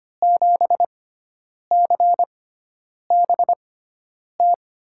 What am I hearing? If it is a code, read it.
Morse code, heard as a keyed tone, about 25 wpm, spaced as for 9 wpm, 700 Hz. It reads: 7CBT
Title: USA Call Signs - 6 Characters - 6x Character Spacing - 25wpm